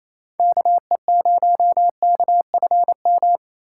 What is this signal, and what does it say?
Morse code, heard as a keyed tone, about 28 wpm, 700 Hz. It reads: KE0KFM
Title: USA Call Signs - 6 Characters - 2x Character Spacing - 28wpm